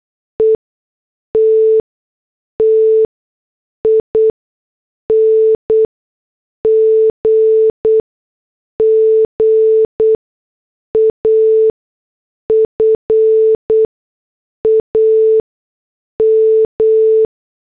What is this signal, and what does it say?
Morse code, heard as a keyed tone, about 8 wpm, spaced as for 6 wpm, 440 Hz. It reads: ETTING GAFAM